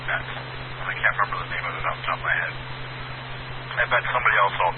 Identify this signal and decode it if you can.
Emergency Alert System (SAME) tones, heard as harsh buzzing bursts, with speech. It is heard from 0.8 to 4.1 s.